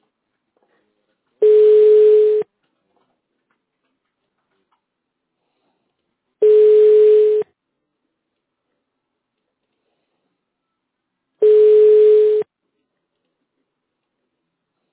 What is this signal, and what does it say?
Ringback tone (Europe, 425 Hz).